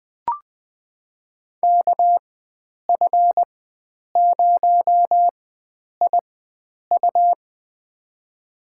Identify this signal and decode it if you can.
Morse code, heard as a keyed tone, about 20 wpm, spaced as for 9 wpm, 700 Hz. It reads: KF0IU
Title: USA Call Signs - 4x Character Spacing - 20wpm